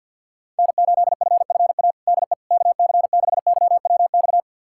Morse code, heard as a keyed tone, about 48 wpm, 700 Hz. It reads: N8FFA DE KC6YWX
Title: USA Callsign DE Callsign 50wpm